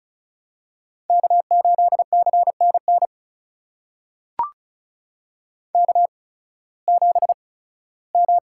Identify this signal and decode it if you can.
Morse code, heard as a keyed tone, about 36 wpm, spaced as for 19 wpm, 700 Hz. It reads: K8CNN K7M
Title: USA Call Signs - 5 Characters - 8x Character Spacing - 35wpm